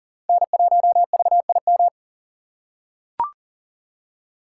Morse code, heard as a keyed tone, about 40 wpm, 700 Hz. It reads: N1VIM
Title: USA Call Signs - 5 Characters - 6x Character Spacing - 40wpm